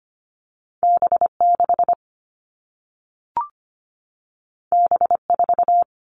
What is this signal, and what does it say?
Morse code, heard as a keyed tone, about 25 wpm, 700 Hz. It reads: B6 B4
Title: All Cut Numbers - 2-Digit 25wpm